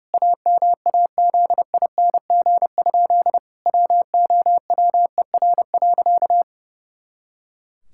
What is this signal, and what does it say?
Morse code, heard as a keyed tone, about 32 wpm, 700 Hz. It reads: AMAZING? WOWER.
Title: Pangram Challenge v2 - 30wpm